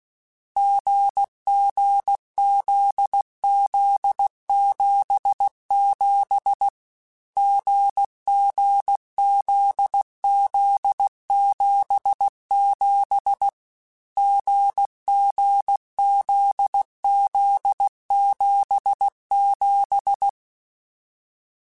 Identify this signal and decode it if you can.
Morse code, heard as a keyed tone, about 16 wpm, 800 Hz. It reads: GGZZ77 GGZZ77 GGZZ77
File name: Short Runs / Short Runs-mp3s / GZ GZ7 @15